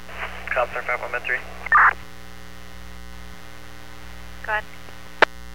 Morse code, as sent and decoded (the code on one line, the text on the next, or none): none